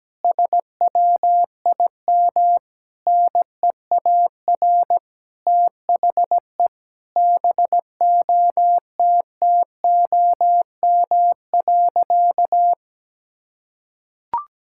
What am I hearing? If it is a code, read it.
Morse code, heard as a keyed tone, about 17 wpm, 700 Hz. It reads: SWIM NEAR THE BOTTOM.
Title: Sentences from Top 1000 Words - Review Part 6 - 17wpm